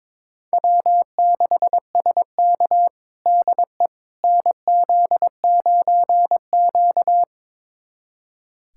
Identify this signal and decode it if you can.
Morse code, heard as a keyed tone, about 21 wpm, 700 Hz. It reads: W6SK DE NZ9Q
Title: USA Callsign DE Callsign - 4 Characters - 22wpm